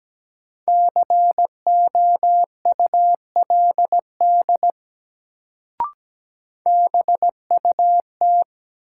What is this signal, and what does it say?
Morse code, heard as a keyed tone, about 17 wpm, 700 Hz. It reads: COULD BUT
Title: Sentences from Top 100 Words - New Word Drill - 17wpm